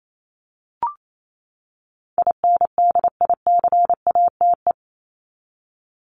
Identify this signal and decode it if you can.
Morse code, heard as a keyed tone, about 28 wpm, 700 Hz. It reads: INDICATE
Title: Top 1251-1500 Words 28wpm